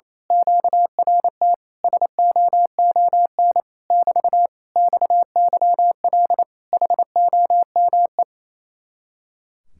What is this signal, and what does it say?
Morse code, heard as a keyed tone, about 26 wpm, 700 Hz. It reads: QRT SOON <BT> XYL HOME
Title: Speed-Racing Common QSO Phrases Encore - 28wpm